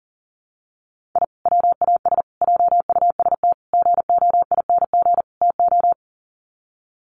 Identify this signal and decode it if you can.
Morse code, heard as a keyed tone, about 37 wpm, 700 Hz. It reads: I WAS JUST GOING TO
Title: Sets of 5 Words Encore 40wpm